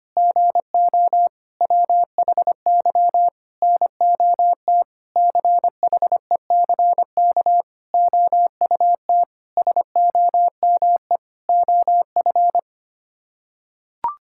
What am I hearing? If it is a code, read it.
Morse code, heard as a keyed tone, about 25 wpm, 700 Hz. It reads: GO WHY NOT CHECK OUT SOME OF
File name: Sets of 9 Words - 25wpm